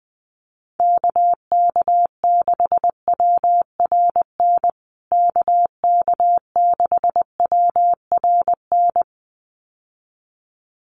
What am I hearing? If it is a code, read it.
Morse code, heard as a keyed tone, about 20 wpm, 700 Hz. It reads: KK6WRN KK6WRN
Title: USA Call Signs Repeated 2x 20wpm